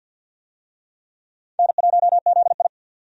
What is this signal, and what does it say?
Morse code, heard as a keyed tone, about 51 wpm, 700 Hz. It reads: N1ZI
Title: USA Call Signs - 4 Characters - 50wpm